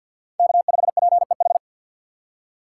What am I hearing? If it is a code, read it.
Morse code, heard as a keyed tone, about 52 wpm, 700 Hz. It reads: K5PEH
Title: USA Call Signs - 5 Characters - 2x Character Spacing - 50wpm